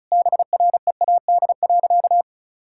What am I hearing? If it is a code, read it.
Morse code, heard as a keyed tone, about 36 wpm, 700 Hz. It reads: BREAD.
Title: Sentences from Top 1600 Words 35wpm